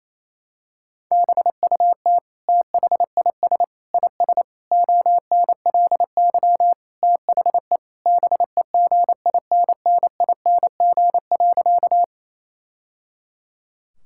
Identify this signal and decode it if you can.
Morse code, heard as a keyed tone, about 28 wpm, 700 Hz. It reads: BUT THIS IS ONLY THE BEGINNING.
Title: Sentences from Top 1000 Words - Review Part 5 - 28wpm